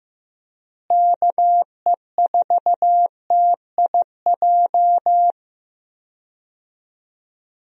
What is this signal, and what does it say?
Morse code, heard as a keyed tone, about 15 wpm, 700 Hz. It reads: KE4TIJ
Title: USA Call Signs - 6 Characters - Repeated 2x - 15wpm